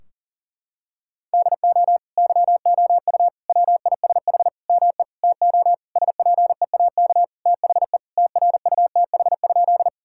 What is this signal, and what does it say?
Morse code, heard as a keyed tone, about 40 wpm, 700 Hz. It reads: DO YOU WISH ME TO SPEAK THE TRUTH?
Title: Sentences from Top 1600 Words 40wpm